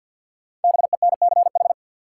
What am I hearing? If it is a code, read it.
Morse code, heard as a keyed tone, about 53 wpm, 700 Hz. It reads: BENCH
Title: Sentences from Top 1400 Words - New Word Drill Alt - 50wpm